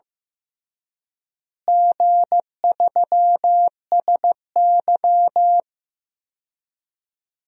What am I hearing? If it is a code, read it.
Morse code, heard as a keyed tone, about 15 wpm, 700 Hz. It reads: G3SY
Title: DXCC-100 Call Signs - 4 Characters - Repeated 2x - 15wpm